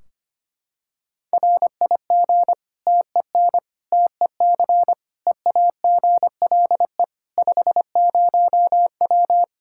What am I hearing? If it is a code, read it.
Morse code, heard as a keyed tone, about 25 wpm, 700 Hz. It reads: RIG TEN TEC EAGLE 50W
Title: Practice QSO - Exchange 2 - Info - 25wpm